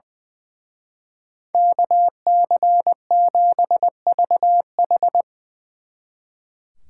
Morse code, heard as a keyed tone, about 22 wpm, 700 Hz. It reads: KC7VH
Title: Speed Racing - USA 2x2 Call Signs 20wpm